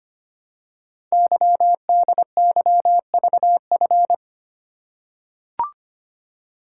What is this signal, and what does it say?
Morse code, heard as a keyed tone, about 25 wpm, 700 Hz. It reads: YDYVF
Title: Code Groups - 5 Letters - 25wpm